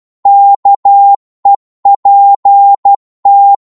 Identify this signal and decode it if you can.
Morse code, heard as a keyed tone, about 12 wpm, 800 Hz. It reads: KEPT